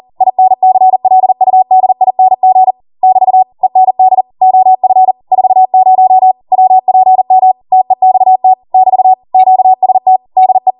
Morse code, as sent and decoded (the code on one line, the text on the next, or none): .. -. -.-. .-.. ..- -.. .. -. --. / -...- / . -. -.. / --- ..-. / ....- ----- / .-- .--. -- / - . -..- - / -...- / --.- ... - / -.. .
INCLUDING <BT> END OF 40 WPM TEXT <BT> QST DE